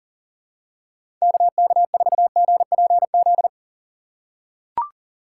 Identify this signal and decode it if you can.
Morse code, heard as a keyed tone, about 40 wpm, 700 Hz. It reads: KK4GPZ
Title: USA Call Signs - 6 Characters - 2x Character Spacing - 40wpm